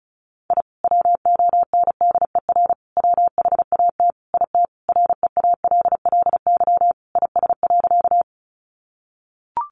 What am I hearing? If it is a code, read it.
Morse code, heard as a keyed tone, about 34 wpm, 700 Hz. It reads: I WONDER WHAT IT REALLY IS.
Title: Sentences from Top 600 Words 35wpm